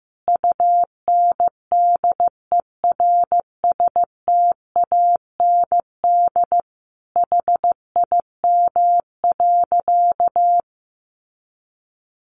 Morse code, heard as a keyed tone, about 14 wpm, 700 Hz. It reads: UNDERSTAND HIM.
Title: Sentences from Top 600 Words 15wpm